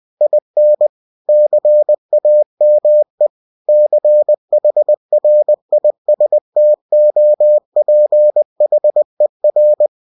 Morse code, heard as a keyed tone, about 20 wpm, 600 Hz. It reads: IN CAME CHRISTOPHER
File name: cw-books / winnie-the-pooh / winnie-20wpm-page-0073